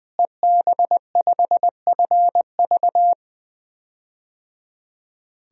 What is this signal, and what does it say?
Morse code, heard as a keyed tone, about 20 wpm, 700 Hz. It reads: EB5FV
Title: DXCC-100 Call Signs - 5 Characters - Repeated 3x - 20wpm